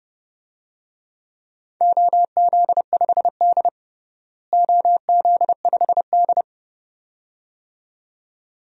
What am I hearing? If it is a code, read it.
Morse code, heard as a keyed tone, about 32 wpm, 700 Hz. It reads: OZ5D OZ5D
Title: DXCC-100 Call Signs - 4 Characters - Repeated 2x - 30wpm